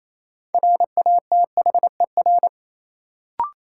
Morse code, heard as a keyed tone, about 28 wpm, 700 Hz. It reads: RATHER
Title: Sentences from Top 700 Words - New Word Drill - 28wpm